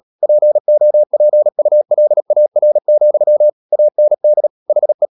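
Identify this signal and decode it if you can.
Morse code, heard as a keyed tone, about 37 wpm, 600 Hz. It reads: POPULAR, AND HE
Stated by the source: Morse code practice at 35–39 wpm sending English text of Ken Follett's novel Never, from CW med Gnister.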